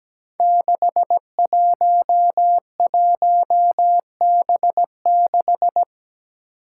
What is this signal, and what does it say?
Morse code, heard as a keyed tone, about 17 wpm, 700 Hz. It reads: 611B6
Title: Five Challenging Letter-Number Combinations 17wpm